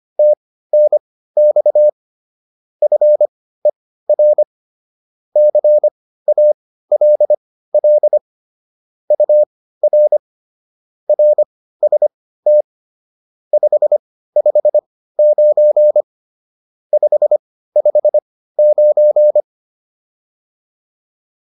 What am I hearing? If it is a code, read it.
Morse code, heard as a keyed tone, about 25 wpm, spaced as for 15 wpm, 600 Hz. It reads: TNX FER CALL UR RST 559 559